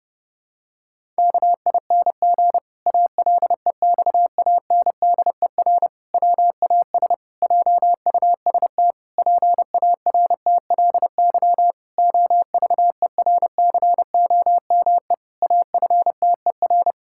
Morse code, heard as a keyed tone, about 30 wpm, 700 Hz. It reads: KING ALEXANDER WAS JUST PARTLY OVERCOME AFTER
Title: Pangram Challenge v2 - 30wpm